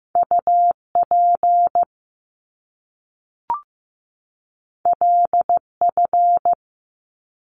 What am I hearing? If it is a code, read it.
Morse code, heard as a keyed tone, about 15 wpm, 700 Hz. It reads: UP LF
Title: Top Two Letter Combinations in Words 15wpm